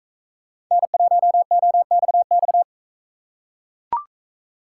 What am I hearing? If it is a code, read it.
Morse code, heard as a keyed tone, about 39 wpm, 700 Hz. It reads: N1OXX